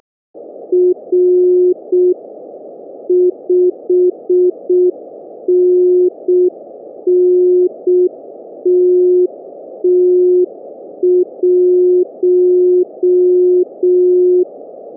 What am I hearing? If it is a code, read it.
Morse code, heard as a keyed tone, about 6 wpm, 360 Hz. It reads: R 5NN TT1